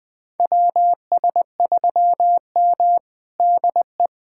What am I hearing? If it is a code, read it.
Morse code, heard as a keyed tone, about 21 wpm, 700 Hz. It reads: WS3M DE
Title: USA Callsign DE Callsign - 4 Characters - 20wpm